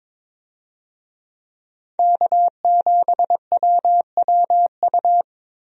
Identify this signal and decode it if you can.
Morse code, heard as a keyed tone, about 22 wpm, 700 Hz. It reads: K7WWU
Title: USA Call Signs - 5 Characters - Repeated 2x - 22wpm